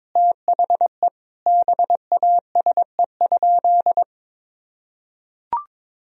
Morse code, heard as a keyed tone, about 22 wpm, 700 Hz. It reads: THE BASE?
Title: Sentences from Top 1000 Words - Review Part 1 - 22wpm